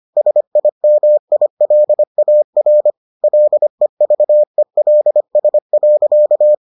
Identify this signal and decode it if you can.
Morse code, heard as a keyed tone, about 25 wpm, 600 Hz. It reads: SIMILAR LEVELS.